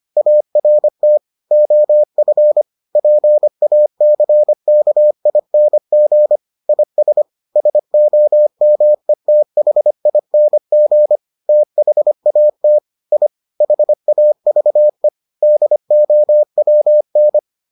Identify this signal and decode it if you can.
Morse code, heard as a keyed tone, about 25 wpm, 600 Hz. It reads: ART OF PACKING IS SOMETHING THAT I HAVE DOWN